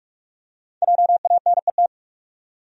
Morse code, heard as a keyed tone, about 44 wpm, 700 Hz. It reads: JANET
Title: USA Common Names - Max 5 Letters - 45wpm